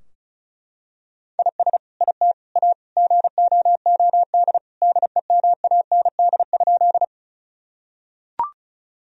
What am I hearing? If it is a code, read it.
Morse code, heard as a keyed tone, about 34 wpm, 700 Hz. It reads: IS IT A GOOD DEMAND?
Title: Sentences from Top 1000 Words - Review Part 4 - 35wpm